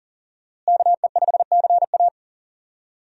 Morse code, heard as a keyed tone, about 40 wpm, 700 Hz. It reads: KE5CA
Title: USA Call Signs - 5 Characters - Repeated 2x - 40wpm